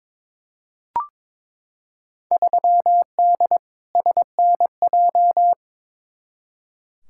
Morse code, heard as a keyed tone, about 22 wpm, 700 Hz. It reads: 3D SNJ
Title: Field Day Exchanges Encore - 22wpm